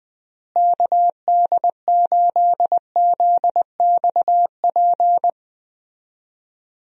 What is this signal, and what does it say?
Morse code, heard as a keyed tone, about 20 wpm, 700 Hz. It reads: KD8ZXP
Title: USA Call Signs - 6 Characters - Repeated 3x - 20wpm